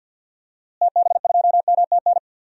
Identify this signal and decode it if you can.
Morse code, heard as a keyed tone, about 47 wpm, 700 Hz. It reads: TB2MTN
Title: DXCC-100 Call Signs - 6 Characters - Repeated 2x - 50wpm